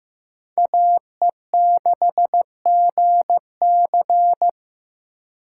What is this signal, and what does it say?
Morse code, heard as a keyed tone, about 18 wpm, 700 Hz. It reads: AE6GC
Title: Speed Racing - USA Call Signs 15wpm